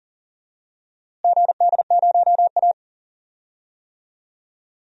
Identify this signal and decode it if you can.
Morse code, heard as a keyed tone, about 40 wpm, 700 Hz. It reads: GD0A
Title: DXCC-100 Call Signs - 4 Characters - Repeated 2x - 40wpm